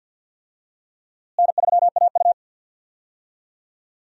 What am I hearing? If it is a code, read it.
Morse code, heard as a keyed tone, about 51 wpm, 700 Hz. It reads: N3AU
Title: USA Call Signs - 4 Characters - Repeated 3x - 50wpm